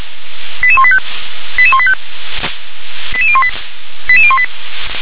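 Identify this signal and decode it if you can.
Five-tone selective call (ZVEI), heard as quick runs of five tones, tones of about 70 ms. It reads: 88185 88185 88018 88018
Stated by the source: ZVEI five-tone selective call 88625 received with an SDR.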